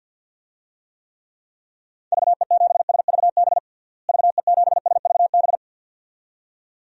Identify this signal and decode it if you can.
Morse code, heard as a keyed tone, about 51 wpm, 700 Hz. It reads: VE7SVB VE7SVB
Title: DXCC-100 Call Signs - 6 Characters - Repeated 2x - 50wpm